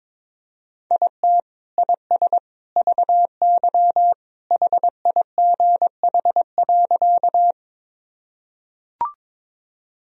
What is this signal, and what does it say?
Morse code, heard as a keyed tone, about 22 wpm, 700 Hz. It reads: IT IS VY HIGH.
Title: Sentences from Top 200 Words and Abbreviations 22wpm